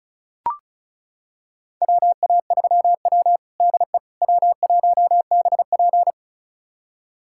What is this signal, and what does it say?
Morse code, heard as a keyed tone, about 34 wpm, 700 Hz. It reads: WA3W DE W1BP
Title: USA Callsign DE Callsign - 4 Characters - 35wpm